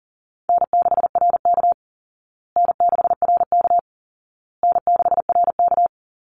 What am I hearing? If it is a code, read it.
Morse code, heard as a keyed tone, about 40 wpm, 700 Hz. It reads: N6RK N6RK N6RK
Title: Easy Call Signs Repeated 3x - 40wpm